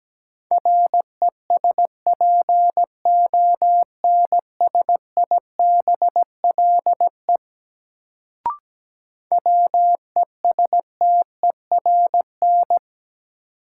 Morse code, heard as a keyed tone, about 17 wpm, 700 Hz. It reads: RESPONSIBLE WESTERN
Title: Sentences from Top 1000 Words - Review Part 7 - New Word Drill - 17wpm